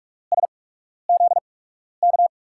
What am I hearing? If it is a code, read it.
Morse code, heard as a keyed tone, about 44 wpm, spaced as for 12 wpm, 700 Hz. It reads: SZK
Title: USA Call Signs - 6 Characters - 8x Character Spacing - 45wpm